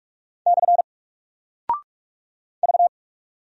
Morse code, heard as a keyed tone, about 48 wpm, 700 Hz. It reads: / V
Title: Single Character (Letter:Number:Punctuation:Prosign) Alt 45wpm